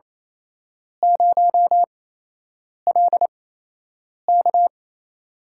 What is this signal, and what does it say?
Morse code, heard as a keyed tone, about 28 wpm, spaced as for 8 wpm, 700 Hz. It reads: 0LK
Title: USA Call Signs - 4 Characters - 8x Character Spacing - 28wpm